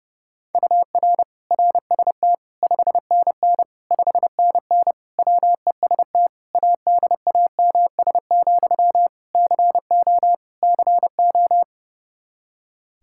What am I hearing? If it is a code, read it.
Morse code, heard as a keyed tone, about 30 wpm, 700 Hz. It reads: UR RST 5NN 5NN WEST ADAMS, CO CO
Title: Practice QSO - Exchange 1 - Beginning - 30wpm